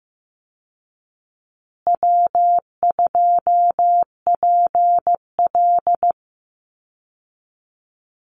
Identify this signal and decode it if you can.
Morse code, heard as a keyed tone, about 15 wpm, 700 Hz. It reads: W2PL
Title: Easy Call Signs Repeated 3x - 15wpm